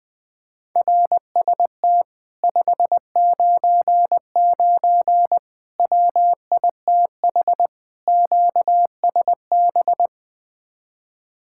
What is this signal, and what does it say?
Morse code, heard as a keyed tone, about 20 wpm, 700 Hz. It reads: RST 599 WITH QSB
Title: Speed-Racing Common QSO Phrases - 17wpm